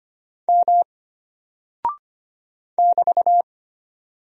Morse code, heard as a keyed tone, about 25 wpm, 700 Hz. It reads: M <BT>
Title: Single Character (Letter:Number:Punctuation:Prosign) Alt 25wpm